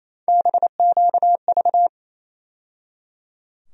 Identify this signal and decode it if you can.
Morse code, heard as a keyed tone, about 27 wpm, 700 Hz. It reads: BQV
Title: USA Call Sign Postfixes - 3 Letters 28wpm